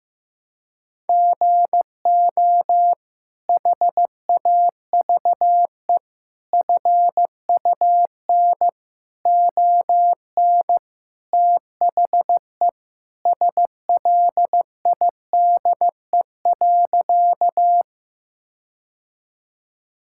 Morse code, heard as a keyed tone, about 15 wpm, 700 Hz. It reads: GO HAVE FUN ON THE SLIDE.
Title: Sentences from Top 1600 Words 15wpm